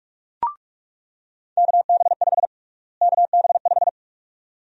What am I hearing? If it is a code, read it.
Morse code, heard as a keyed tone, about 47 wpm, 700 Hz. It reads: KB5 KB5
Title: USA Call Sign Prefixes - Repeated 2x - 45wpm